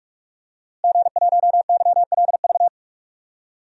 Morse code, heard as a keyed tone, about 46 wpm, 700 Hz. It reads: G1YLV
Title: DXCC-100 Call Signs - 5 Characters - Repeated 3x - 45wpm